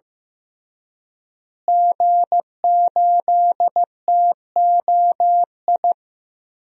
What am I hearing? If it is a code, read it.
Morse code, heard as a keyed tone, about 16 wpm, 700 Hz. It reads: G8TOI
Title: DXCC-100 Call Signs - 5 Characters - Repeated 3x - 15wpm